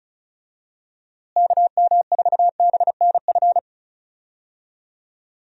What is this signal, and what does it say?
Morse code, heard as a keyed tone, about 34 wpm, 700 Hz. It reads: KM4BNF
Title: USA Call Signs - 6 Characters - Repeated 3x - 35wpm